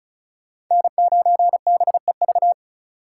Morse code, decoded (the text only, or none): N9BEV